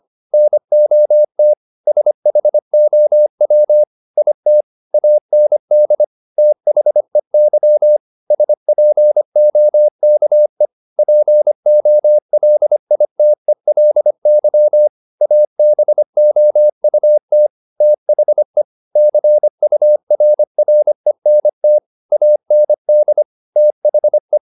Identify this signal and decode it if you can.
Morse code, heard as a keyed tone, about 25 wpm, 600 Hz. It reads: NOT SHOW IT AND THEY SPOKE POLITELY ABOUT THE CURRENT AND THE